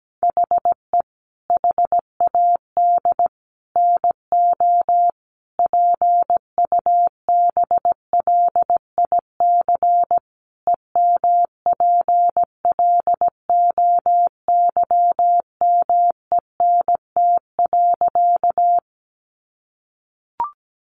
Morse code, heard as a keyed tone, about 17 wpm, 700 Hz. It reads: HE HAD NO PUBLIC EMPLOYMENT.
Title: Sentences from Top 1500 Words 17wpm